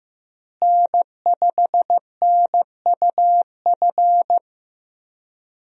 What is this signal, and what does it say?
Morse code, heard as a keyed tone, about 15 wpm, 700 Hz. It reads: N5NUF
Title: USA Call Signs - 5 Characters - Repeated 3x - 15wpm